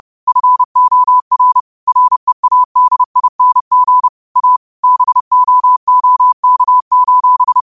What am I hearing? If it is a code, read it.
Morse code, heard as a keyed tone, about 30 wpm, 1000 Hz. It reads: FOR READING A BOOK:
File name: qotdfiles / 30wpm